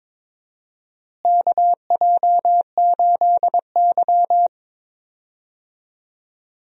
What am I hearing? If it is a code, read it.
Morse code, heard as a keyed tone, about 22 wpm, 700 Hz. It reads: KJ8Y
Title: USA Call Signs - 4 Characters - Repeated 2x - 22wpm